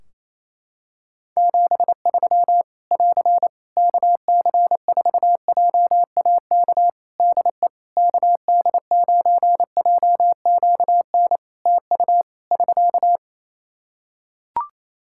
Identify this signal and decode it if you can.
Morse code, heard as a keyed tone, about 28 wpm, 700 Hz. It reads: 73 <AR> KC4JAK DE KD9JQN TU <SK>
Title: Practice QSO - Exchange 4 - Ending - 28wpm